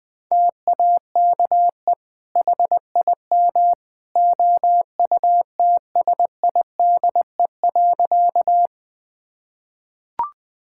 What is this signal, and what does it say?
Morse code, heard as a keyed tone, about 20 wpm, 700 Hz. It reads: TAKE HIM OUTSIDE.
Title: Sentences from Top 1000 Words - Review Part 6 - 20wpm